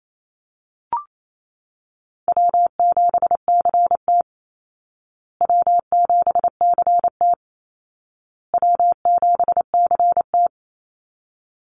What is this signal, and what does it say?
Morse code, heard as a keyed tone, about 28 wpm, 700 Hz. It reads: W7CT W7CT W7CT
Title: Easy Call Signs Repeated 3x - 28wpm